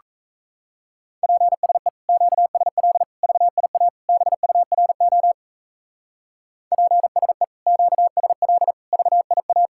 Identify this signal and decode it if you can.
Morse code, heard as a keyed tone, about 32 wpm, 700 Hz. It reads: PSE QSL VIA BURO PSE QSL VIA
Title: Speed-Racing Common QSO Phrases - 28wpm